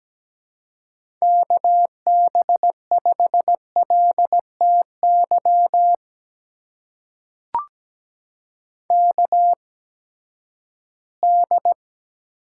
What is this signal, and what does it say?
Morse code, heard as a keyed tone, about 17 wpm, spaced as for 12 wpm, 700 Hz. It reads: KB5LTY KD
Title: USA Call Signs - 6 Characters - 8x Character Spacing - 17wpm